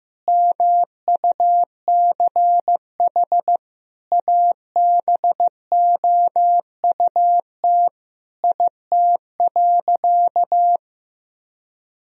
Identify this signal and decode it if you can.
Morse code, heard as a keyed tone, about 15 wpm, 700 Hz. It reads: MUCH ABOUT IT.
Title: Sentences from Top 1000 Words - Review Part 2 - 15wpm